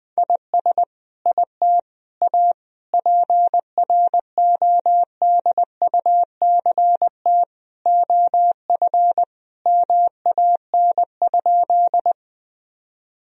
Morse code, heard as a keyed tone, about 20 wpm, 700 Hz. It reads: IS IT A PRODUCT OF MAN?
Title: Sentences from Top 1000 Words - Review Part 2 - 20wpm